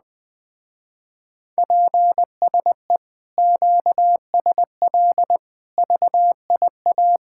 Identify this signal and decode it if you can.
Morse code, heard as a keyed tone, about 21 wpm, 700 Hz. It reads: PSE QSL VIA
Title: Speed-Racing Common QSO Phrases - 17wpm